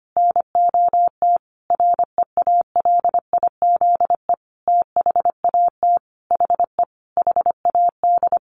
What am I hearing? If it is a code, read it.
Morse code, heard as a keyed tone, about 25 wpm, 700 Hz. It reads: NOT REALIZE THAT HE HAD
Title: Sentences from Top 1000 Words 25wpm